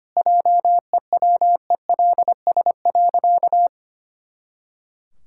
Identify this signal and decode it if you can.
Morse code, heard as a keyed tone, about 25 wpm, 700 Hz. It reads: JEWELS.
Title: Pangram Challenge v2 - 25wpm